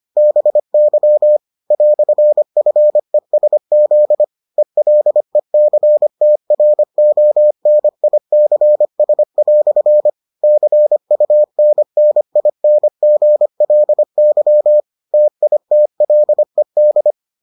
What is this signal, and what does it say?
Morse code, heard as a keyed tone, about 25 wpm, 600 Hz. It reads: BY "FESZ ELECTRONICS" CUNNINGLY TITLED